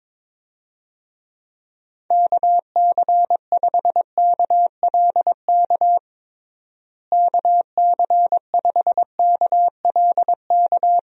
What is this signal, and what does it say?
Morse code, heard as a keyed tone, about 22 wpm, 700 Hz. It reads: KC5KLK KC5KLK
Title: USA Call Signs - 6 Characters - Repeated 2x - 22wpm